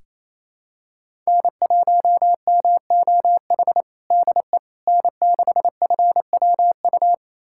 Morse code, heard as a keyed tone, about 28 wpm, 700 Hz. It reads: N1MOH DE N6FWU
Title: USA Callsign DE Callsign - 5 Characters - 28wpm